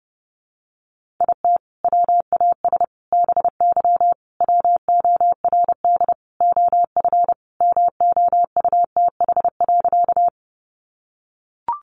Morse code, heard as a keyed tone, about 30 wpm, 700 Hz. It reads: IT WAS BY WORD OF MOUTH.
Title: Sentences from Top 1500 Words 30wpm